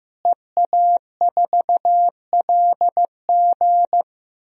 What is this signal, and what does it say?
Morse code, heard as a keyed tone, about 15 wpm, 700 Hz. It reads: EA4LG
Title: DXCC-100 Call Signs - 5 Characters - Repeated 3x - 15wpm